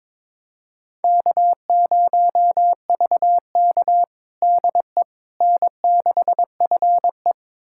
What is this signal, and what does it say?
Morse code, heard as a keyed tone, about 22 wpm, 700 Hz. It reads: K0VK DE N6FE
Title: USA Callsign DE Callsign - 4 Characters - 22wpm